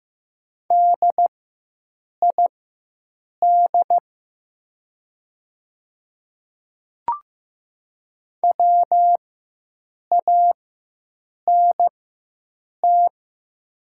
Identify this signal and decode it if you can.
Morse code, heard as a keyed tone, about 15 wpm, spaced as for 7 wpm, 700 Hz. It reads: DID WANT
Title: 4x Character Spacing - Top 200 Words - 15wpm